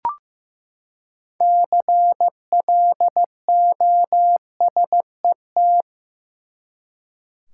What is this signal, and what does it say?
Morse code, heard as a keyed tone, about 15 wpm, 700 Hz. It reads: CLOSET
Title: Six Letter Words - 15wpm